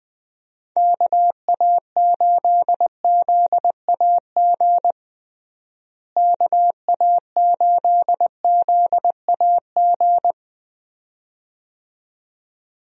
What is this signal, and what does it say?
Morse code, heard as a keyed tone, about 20 wpm, 700 Hz. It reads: KA8ZAG KA8ZAG
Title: USA Call Signs - 6 Characters - Repeated 2x - 20wpm